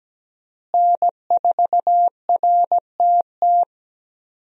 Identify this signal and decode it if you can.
Morse code, heard as a keyed tone, about 17 wpm, 700 Hz. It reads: N4RTT
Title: USA Call Signs - 5 Characters - Repeated 2x - 17wpm